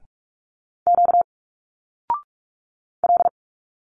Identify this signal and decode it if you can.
Morse code, heard as a keyed tone, about 46 wpm, 700 Hz. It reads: Q L
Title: Single Letters 45wpm